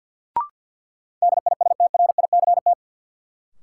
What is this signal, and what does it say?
Morse code, heard as a keyed tone, about 53 wpm, 700 Hz. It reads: DISTRICT
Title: Sentences from Top 1600 Words - New Word Drill - 50wpm